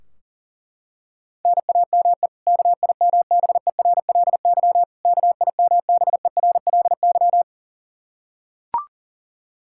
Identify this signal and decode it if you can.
Morse code, heard as a keyed tone, about 40 wpm, 700 Hz. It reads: NAME KIMBERLY KIMBERLY
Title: Practice QSO - Exchange 1 - Beginning - 40wpm